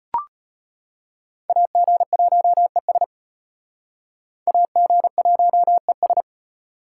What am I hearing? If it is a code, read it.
Morse code, heard as a keyed tone, about 34 wpm, 700 Hz. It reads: AG1ES AG1ES